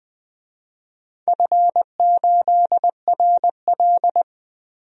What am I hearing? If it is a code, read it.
Morse code, heard as a keyed tone, about 20 wpm, 700 Hz. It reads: F8RL
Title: DXCC-100 Call Signs - 4 Characters - Repeated 2x - 20wpm